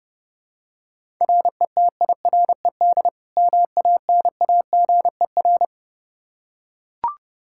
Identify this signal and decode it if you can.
Morse code, heard as a keyed tone, about 30 wpm, 700 Hz. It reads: RETIRED MANAGER
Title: Practice QSO - Exchange 3 - More Info - 30wpm